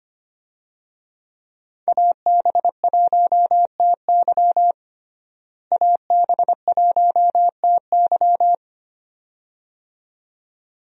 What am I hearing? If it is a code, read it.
Morse code, heard as a keyed tone, about 25 wpm, 700 Hz. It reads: AB1TY AB1TY
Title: USA Call Signs - 5 Characters - Repeated 2x - 25wpm